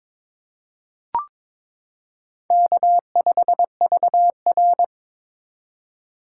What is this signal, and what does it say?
Morse code, heard as a keyed tone, about 22 wpm, 700 Hz. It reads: K5VR
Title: K1USN SST QSO Practice - 22wpm